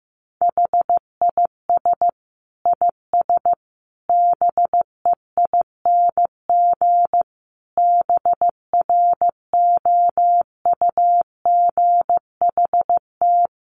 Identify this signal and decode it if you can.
Morse code, heard as a keyed tone, about 15 wpm, 700 Hz. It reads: HIS IS BEING BROUGHT
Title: Sentences from Top 600 Words 15wpm